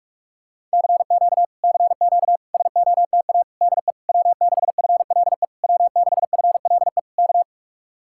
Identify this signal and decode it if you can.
Morse code, heard as a keyed tone, about 49 wpm, 700 Hz. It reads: CQ CQ SOTA DE W6FLE W6FLE K